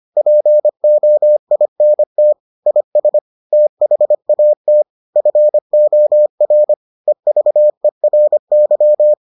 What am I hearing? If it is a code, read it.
Morse code, heard as a keyed tone, about 25 wpm, 600 Hz. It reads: POINT IS THAT FOR EVERY